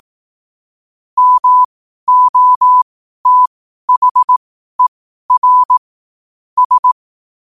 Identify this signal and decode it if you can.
Morse code, heard as a keyed tone, about 18 wpm, spaced as for 12 wpm, 1000 Hz. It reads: MOTHER S
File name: qotdfiles / 12wpm